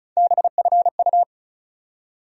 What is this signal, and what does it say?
Morse code, heard as a keyed tone, about 36 wpm, 700 Hz. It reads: BFU
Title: USA Call Sign Postfixes - 3 Letters 35wpm